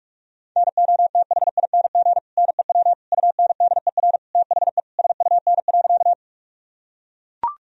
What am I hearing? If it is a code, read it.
Morse code, heard as a keyed tone, about 41 wpm, 700 Hz. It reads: NOTHING NEW UNDER THE SUN.
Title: Sentences from Top 1000 Words - Review Part 1 - 45wpm